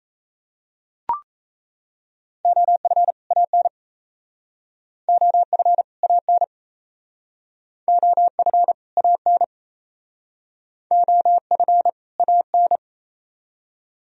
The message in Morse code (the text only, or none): OF AN OF AN OF AN OF AN